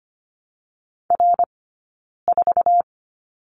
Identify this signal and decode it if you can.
Morse code, heard as a keyed tone, about 25 wpm, 700 Hz. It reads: R 4
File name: Lesson 17d - Letter u - Two Characters 25wpm